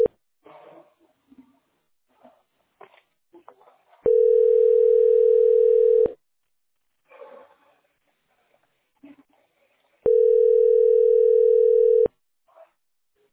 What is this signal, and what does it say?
Ringback tone (North America), 440 + 480 Hz.